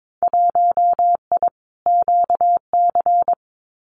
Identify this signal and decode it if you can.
Morse code, heard as a keyed tone, about 22 wpm, 700 Hz. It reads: 1I QC
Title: Winter Field Day Exchanges 22wpm